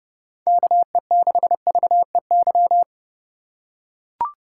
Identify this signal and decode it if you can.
Morse code, heard as a keyed tone, about 30 wpm, 700 Hz. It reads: KE6VEY
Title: USA Call Signs - 2x Character Spacing - 30wpm